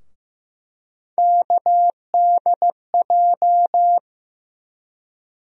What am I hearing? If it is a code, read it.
Morse code, heard as a keyed tone, about 15 wpm, 700 Hz. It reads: KDJ
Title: USA Call Sign Postfixes - 3 Letters 15wpm